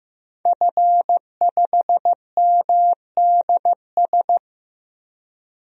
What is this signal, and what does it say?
Morse code, heard as a keyed tone, about 15 wpm, 700 Hz. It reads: F5MDS
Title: DXCC-100 Call Signs - 5 Characters - Repeated 3x - 15wpm